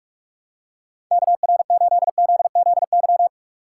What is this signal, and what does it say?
Morse code, heard as a keyed tone, about 46 wpm, 690 Hz. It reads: KR8ZZY
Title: USA Call Signs - 6 Characters - 2x Character Spacing - 45wpm